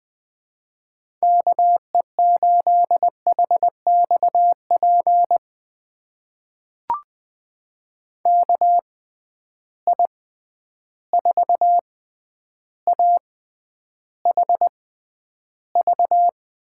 Morse code, heard as a keyed tone, about 20 wpm, spaced as for 10 wpm, 700 Hz. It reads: KE8HXP KI4AHV
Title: USA Call Signs - 6 Characters - 6x Character Spacing - 20wpm